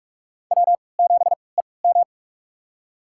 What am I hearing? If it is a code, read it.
Morse code, heard as a keyed tone, about 44 wpm, spaced as for 25 wpm, 700 Hz. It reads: W7EM
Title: USA Call Signs - 4 Characters - 3x Character Spacing - 45wpm